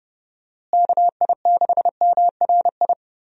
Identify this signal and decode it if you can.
Morse code, heard as a keyed tone, about 30 wpm, 700 Hz. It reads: KI6MRI
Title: USA Call Signs - 6 Characters - Repeated 3x - 30wpm